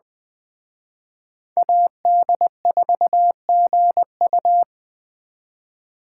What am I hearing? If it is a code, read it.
Morse code, heard as a keyed tone, about 20 wpm, 700 Hz. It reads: AD4GU